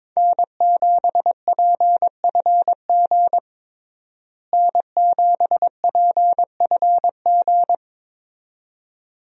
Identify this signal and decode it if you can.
Morse code, heard as a keyed tone, about 22 wpm, 700 Hz. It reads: N7PFG N7PFG
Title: USA Call Signs - 5 Characters - Repeated 2x - 22wpm